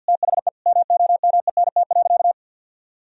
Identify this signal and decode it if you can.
Morse code, heard as a keyed tone, about 50 wpm, 700 Hz. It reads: THE MOMENT.
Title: Sentences from Top 1000 Words - Review Part 3 - 50wpm